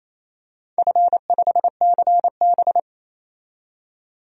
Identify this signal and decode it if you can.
Morse code, heard as a keyed tone, about 28 wpm, 700 Hz. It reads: F5CB
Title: DXCC-100 Call Signs - 4 Characters - 28wpm